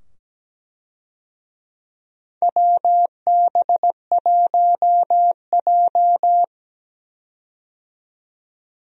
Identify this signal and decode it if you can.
Morse code, heard as a keyed tone, about 17 wpm, 700 Hz. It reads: WB1J
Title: USA Call Signs - 4 Characters - 17wpm